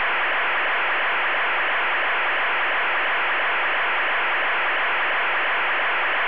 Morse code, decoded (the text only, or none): none